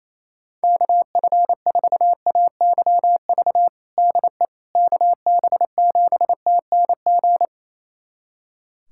Morse code, heard as a keyed tone, about 27 wpm, 700 Hz. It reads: KF4AYV DE KB7TNG
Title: USA Callsign DE Callsign - 6 Characters - 28wpm